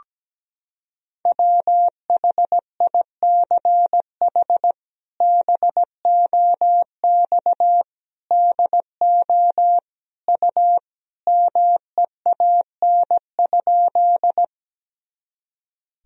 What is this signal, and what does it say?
Morse code, heard as a keyed tone, about 17 wpm, 700 Hz. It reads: WHICH BOX DO U MEAN?
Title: Sentences from Top 500 Words and Abbreviations 17wpm